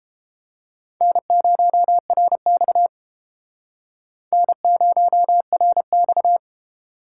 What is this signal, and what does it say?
Morse code, heard as a keyed tone, about 29 wpm, 700 Hz. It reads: N0RX N0RX